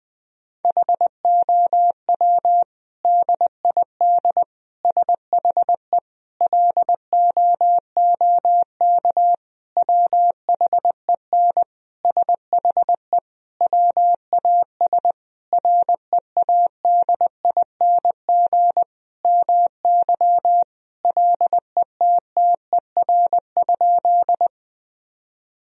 Morse code, heard as a keyed tone, about 19 wpm, 700 Hz. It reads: HOW DID SHE LOOK WHEN SHE WAS READING MY LETTER?
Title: Sentences from Top 1000 Words - Review Part 3 - 20wpm